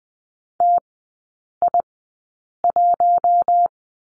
Morse code, heard as a keyed tone, about 20 wpm, spaced as for 8 wpm, 700 Hz. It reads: TI1
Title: 093 Lesson 16e - Letter c - Three Characters 20wpm